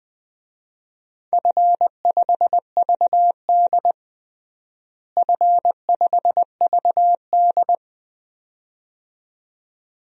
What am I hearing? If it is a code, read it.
Morse code, heard as a keyed tone, about 20 wpm, 700 Hz. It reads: F5VD F5VD
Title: DXCC-100 Call Signs - 4 Characters - Repeated 2x - 20wpm